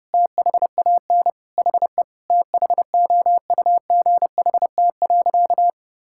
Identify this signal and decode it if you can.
Morse code, heard as a keyed tone, about 32 wpm, 700 Hz. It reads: THAN HE THOUGHT.
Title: Sentences from Top 1600 Words 30wpm